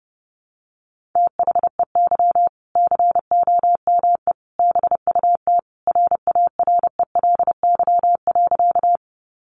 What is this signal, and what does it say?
Morse code, heard as a keyed tone, about 30 wpm, 700 Hz. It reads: THEY COME BUT RARELY.
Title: Sentences from Top 1400 Words 30wpm